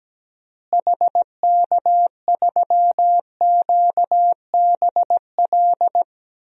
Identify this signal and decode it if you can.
Morse code, heard as a keyed tone, about 17 wpm, 700 Hz. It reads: HK3QBL
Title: DXCC-100 Call Signs - 6 Characters - Repeated 2x - 17wpm